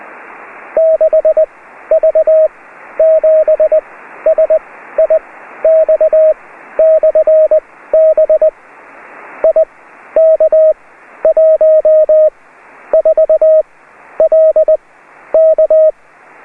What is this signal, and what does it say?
Morse code, heard as a keyed tone, about 20 wpm, spaced as for 12 wpm, 630 Hz. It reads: 6V7SIX/B IK14LK